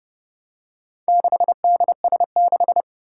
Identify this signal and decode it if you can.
Morse code, heard as a keyed tone, about 30 wpm, 700 Hz. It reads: 6DS6
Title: Four Challenging Letter-Number Combinations 30wpm